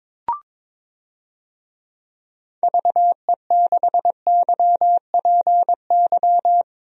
Morse code, heard as a keyed tone, about 22 wpm, 700 Hz. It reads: VE6YPY
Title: DXCC-100 Call Signs - 6 Characters - Repeated 2x - 22wpm